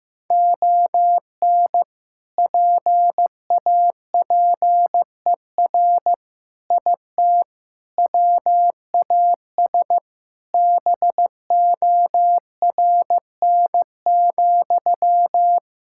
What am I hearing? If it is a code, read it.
Morse code, heard as a keyed tone, about 16 wpm, 700 Hz. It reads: ON PAPER IT WAS BORN,
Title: Sentences from Top 1600 Words 15wpm